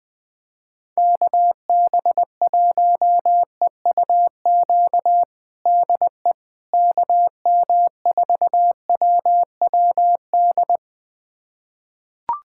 Morse code, heard as a keyed tone, about 20 wpm, 700 Hz. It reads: KB1EUQ DE KM4WWD
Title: USA Callsign DE Callsign - 6 Characters - 20wpm